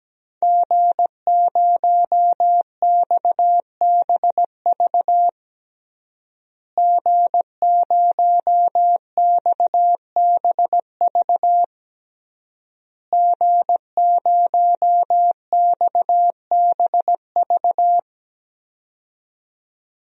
Morse code, heard as a keyed tone, about 17 wpm, 700 Hz. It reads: G0XBV G0XBV G0XBV
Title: DXCC-100 Call Signs - 5 Characters - Repeated 3x - 17wpm